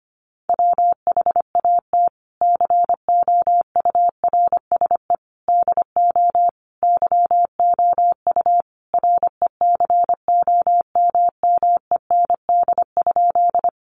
Morse code, heard as a keyed tone, about 25 wpm, 700 Hz. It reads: WHAT COURSE DO YOU RECOMMEND?
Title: Sentences from Top 1200 Words 25wpm